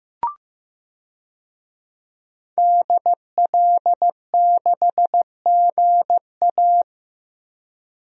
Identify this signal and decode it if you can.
Morse code, heard as a keyed tone, about 15 wpm, 700 Hz. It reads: DL6GA
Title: DXCC-100 Call Signs - 5 Characters - Repeated 2x - 15wpm